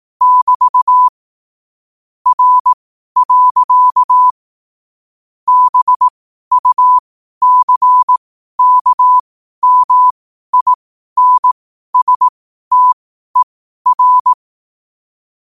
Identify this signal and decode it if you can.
Morse code, heard as a keyed tone, about 18 wpm, spaced as for 12 wpm, 1000 Hz. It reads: <BT> R. BUCKMINSTER
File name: qotdfiles / 12wpm